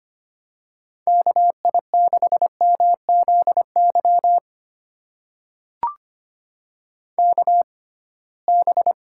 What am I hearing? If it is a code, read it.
Morse code, heard as a keyed tone, about 25 wpm, spaced as for 19 wpm, 700 Hz. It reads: KI6MZY KB
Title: USA Call Signs - 6 Characters - 6x Character Spacing - 25wpm